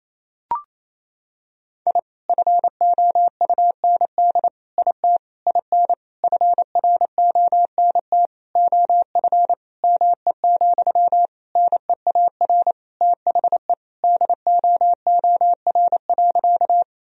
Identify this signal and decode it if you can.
Morse code, heard as a keyed tone, about 28 wpm, 700 Hz. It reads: I FOUND IT IN FRONT OF ME, NEAR THE DOOR.
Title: Sentences from Top 1000 Words - Review Part 2 - 28wpm